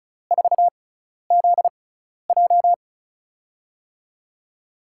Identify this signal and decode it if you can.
Morse code, heard as a keyed tone, about 36 wpm, spaced as for 12 wpm, 700 Hz. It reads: 4ZJ
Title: USA Call Signs - 6x Character Spacing - 35wpm